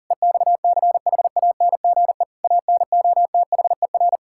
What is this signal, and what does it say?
Morse code, heard as a keyed tone, about 41 wpm, 700 Hz. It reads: EXCHANGE ANOTHER
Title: Sentences from Top 1000 Words - Review Part 6 - 40wpm